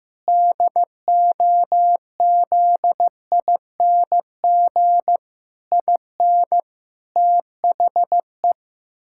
Morse code, heard as a keyed tone, about 16 wpm, 700 Hz. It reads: DOZING IN THE
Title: Pangram Challenge v2 - 15wpm